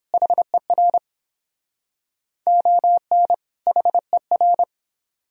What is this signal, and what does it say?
Morse code, heard as a keyed tone, about 31 wpm, 700 Hz. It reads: HER ON HER